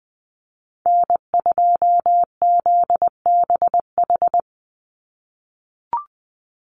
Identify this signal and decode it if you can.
Morse code, heard as a keyed tone, about 20 wpm, 700 Hz. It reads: N2ZBH
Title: USA Call Signs 20wpm